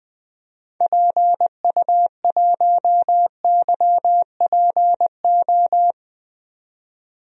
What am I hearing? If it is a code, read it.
Morse code, heard as a keyed tone, about 20 wpm, 700 Hz. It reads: PU1YPO
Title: DXCC-100 Call Signs - 6 Characters - 20wpm